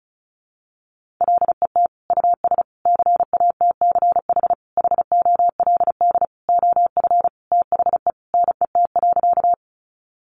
Morse code, heard as a keyed tone, about 34 wpm, 700 Hz. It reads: LET US CATCH HOLD OF THE NET.
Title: Sentences from Top 1500 Words 35wpm